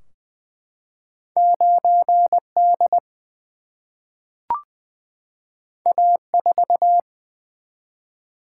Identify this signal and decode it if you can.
Morse code, heard as a keyed tone, about 20 wpm, 700 Hz. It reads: 9D A4
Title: Single-Letter Single-Number 20wpm